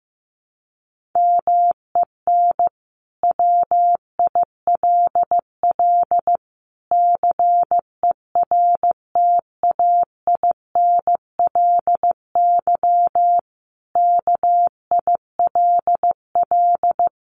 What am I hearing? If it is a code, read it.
Morse code, heard as a keyed tone, about 15 wpm, 700 Hz. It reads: MEN WILL CERTAINLY KILL
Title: Sentences from Top 1500 Words 15wpm